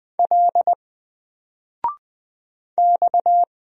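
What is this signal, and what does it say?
Morse code, heard as a keyed tone, about 20 wpm, 700 Hz. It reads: L X
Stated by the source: Single letter and number Morse drill at 20 wpm, from N4LCD.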